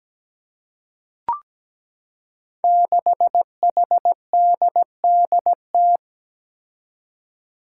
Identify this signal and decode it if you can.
Morse code, heard as a keyed tone, about 17 wpm, 700 Hz. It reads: 6HDDT
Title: Five Challenging Letter-Number Combinations 17wpm